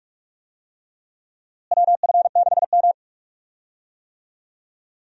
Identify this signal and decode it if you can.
Morse code, heard as a keyed tone, about 46 wpm, 700 Hz. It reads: WF6M
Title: USA Call Signs - 4 Characters - Repeated 3x - 45wpm